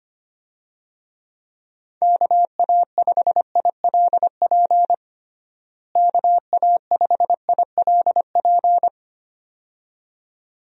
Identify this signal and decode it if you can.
Morse code, heard as a keyed tone, about 25 wpm, 700 Hz. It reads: KA5ILP KA5ILP
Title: USA Call Signs - 6 Characters - Repeated 2x - 25wpm